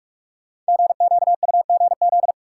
Morse code, heard as a keyed tone, about 44 wpm, 700 Hz. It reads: GQUGZ
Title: Code Groups - 5 Letters - 45wpm